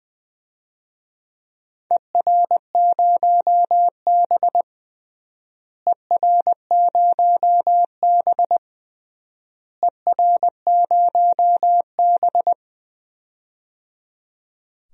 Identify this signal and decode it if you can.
Morse code, heard as a keyed tone, about 20 wpm, 700 Hz. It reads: ER0B ER0B ER0B
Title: DXCC-100 Call Signs - 4 Characters - Repeated 3x - 20wpm